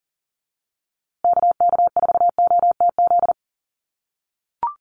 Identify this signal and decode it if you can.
Morse code, heard as a keyed tone, about 38 wpm, 700 Hz. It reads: KK4OTZ
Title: USA Call Signs Repeated 3x 40wpm